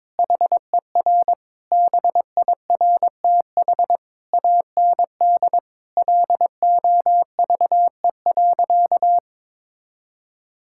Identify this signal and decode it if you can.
Morse code, heard as a keyed tone, about 22 wpm, 700 Hz. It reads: HER BIRTH AND LOVE.
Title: Sentences from Top 1000 Words - Review Part 5 - 22wpm